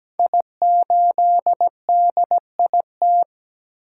Morse code, heard as a keyed tone, about 17 wpm, 700 Hz. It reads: I8DIT
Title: Code Groups - 5 Letter:Numbers - 17wpm